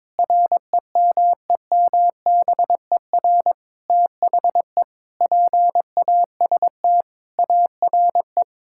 Morse code, heard as a keyed tone, about 22 wpm, 700 Hz. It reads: REMEMBER THE PAST ARE
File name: practice / 22wpm / Sentences with 8-Word Phrases - 22wpm